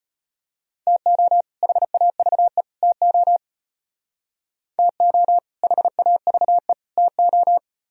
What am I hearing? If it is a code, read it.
Morse code, heard as a keyed tone, about 30 wpm, 700 Hz. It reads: TO HAVE TO TO HAVE TO
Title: Speed-Racing - Sets of 3 Words 25wpm